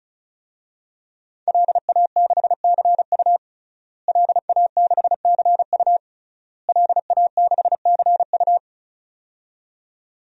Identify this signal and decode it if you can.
Morse code, heard as a keyed tone, about 34 wpm, 700 Hz. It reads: LA6CU LA6CU LA6CU
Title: DXCC-100 Call Signs - 5 Characters - Repeated 3x - 35wpm